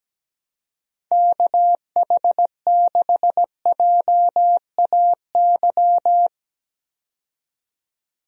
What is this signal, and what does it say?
Morse code, heard as a keyed tone, about 17 wpm, 700 Hz. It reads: KH6JAY
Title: DXCC-100 Call Signs - 6 Characters - 17wpm